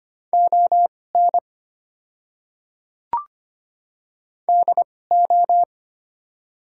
Morse code, heard as a keyed tone, about 25 wpm, spaced as for 18 wpm, 690 Hz. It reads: ON DO
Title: 2x Character Spacing - Top 200 Words - 25wpm